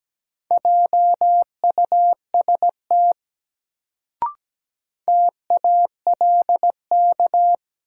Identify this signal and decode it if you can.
Morse code, heard as a keyed tone, about 17 wpm, 700 Hz. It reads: JUST TALK
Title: Sets of 2 Words - New Word Drill - 17wpm